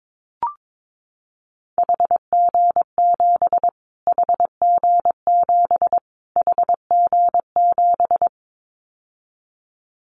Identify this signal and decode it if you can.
Morse code, heard as a keyed tone, about 22 wpm, 700 Hz. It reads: HG7 HG7 HG7
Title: Callsign Prefixes for Top 25 DXCC - Repeated 3x - 22wpm